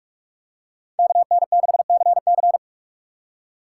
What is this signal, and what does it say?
Morse code, heard as a keyed tone, about 44 wpm, 700 Hz. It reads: KN6CC
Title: USA Call Signs - 5 Characters - 6x Character Spacing - 45wpm